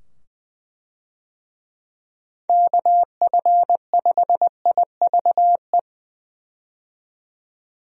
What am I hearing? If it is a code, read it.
Morse code, heard as a keyed tone, about 20 wpm, 700 Hz. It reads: KF5IVE
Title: USA Call Signs - 6 Characters - 20wpm